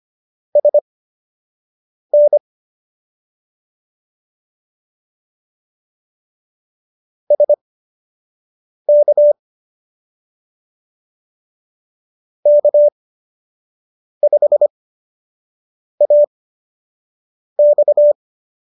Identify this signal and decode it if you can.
Morse code, heard as a keyed tone, about 25 wpm, spaced as for 6 wpm, 590 Hz. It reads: SN SK K5AX